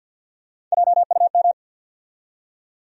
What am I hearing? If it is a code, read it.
Morse code, heard as a keyed tone, about 52 wpm, 700 Hz. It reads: JUM
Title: USA Call Sign Postfixes - 3 Letters 50wpm